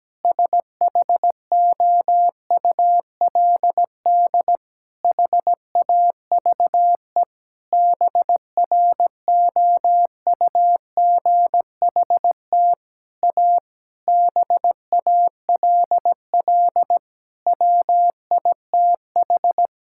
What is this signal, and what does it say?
Morse code, heard as a keyed tone, about 17 wpm, 700 Hz. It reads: SHOULD HAVE BROUGHT A BALL WITH
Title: Sentences from Top 1000 Words - Review Part 6 - 17wpm